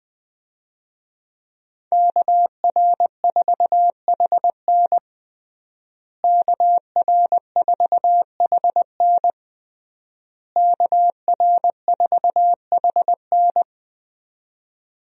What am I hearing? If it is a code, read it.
Morse code, heard as a keyed tone, about 20 wpm, 700 Hz. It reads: KR4HN KR4HN KR4HN
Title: USA Call Signs - 5 Characters - Repeated 3x - 20wpm